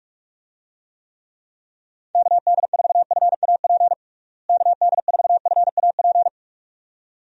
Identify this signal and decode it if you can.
Morse code, heard as a keyed tone, about 46 wpm, 700 Hz. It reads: KD4FAP KD4FAP
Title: USA Call Signs - 6 Characters - Repeated 2x - 45wpm